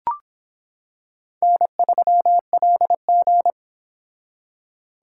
Morse code, heard as a keyed tone, about 24 wpm, 700 Hz. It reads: N3LG